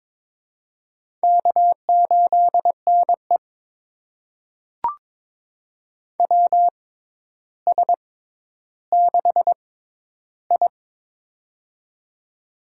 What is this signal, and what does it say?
Morse code, heard as a keyed tone, about 22 wpm, spaced as for 11 wpm, 700 Hz. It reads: K8NE WS6I
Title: USA Call Signs - 4 Characters - 6x Character Spacing - 22wpm